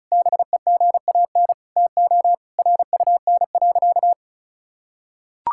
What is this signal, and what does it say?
Morse code, heard as a keyed tone, about 36 wpm, 690 Hz. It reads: BEGAN TO RUN.
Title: Sentences from Top 1000 Words - Review Part 5 - 35wpm